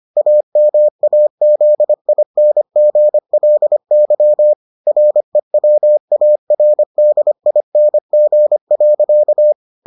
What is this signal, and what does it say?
Morse code, heard as a keyed tone, about 25 wpm, 600 Hz. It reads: AMAZINGLY REWARDING.